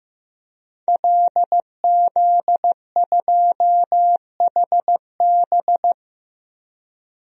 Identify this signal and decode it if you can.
Morse code, heard as a keyed tone, about 15 wpm, 700 Hz. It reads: LZ2HB
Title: DXCC-100 Call Signs - 5 Characters - 15wpm